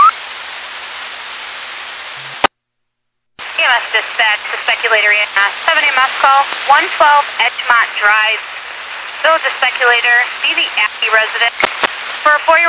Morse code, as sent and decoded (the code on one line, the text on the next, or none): none